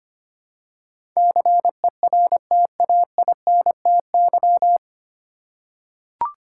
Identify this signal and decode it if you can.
Morse code, heard as a keyed tone, about 25 wpm, 700 Hz. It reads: CERTAINTY
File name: practice / 25wpm / 9 Letter Words - First 3 Letters Given - 25wpm